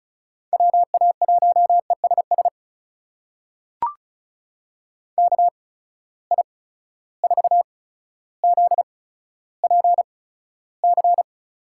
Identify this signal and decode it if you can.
Morse code, heard as a keyed tone, about 34 wpm, spaced as for 15 wpm, 700 Hz. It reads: WA1ESS KI4ZPC
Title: USA Call Signs - 6 Characters - 8x Character Spacing - 35wpm